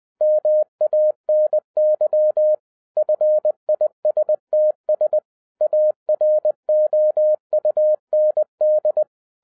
Morse code, heard as a keyed tone, about 20 wpm, 610 Hz. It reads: MANY FISTS AROUND